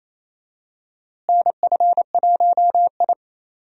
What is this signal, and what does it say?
Morse code, heard as a keyed tone, about 28 wpm, 700 Hz. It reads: NF1I
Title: USA Call Signs - 4 Characters - Repeated 2x - 28wpm